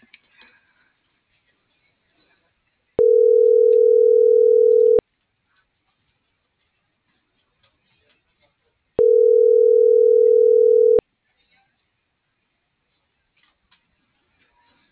Ringback tone (North America), 440 + 480 Hz.